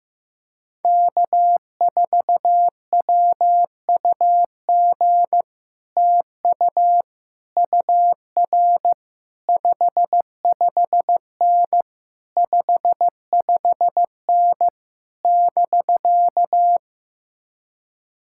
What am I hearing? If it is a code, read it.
Morse code, heard as a keyed tone, about 15 wpm, 700 Hz. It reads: K4WUG TU UR 55N 55N <BK>
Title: Speed-Racing Common QSO Phrases Encore - 15wpm